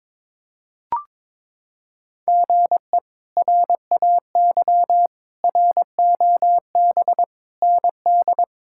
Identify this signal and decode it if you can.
Morse code, heard as a keyed tone, about 22 wpm, 700 Hz. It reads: GE RAY ROB ND
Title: K1USN SST QSO Practice - 22wpm